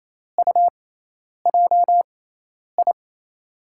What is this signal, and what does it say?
Morse code, heard as a keyed tone, about 28 wpm, spaced as for 10 wpm, 700 Hz. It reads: UJI
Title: USA Call Signs - 5 Characters - 6x Character Spacing - 28wpm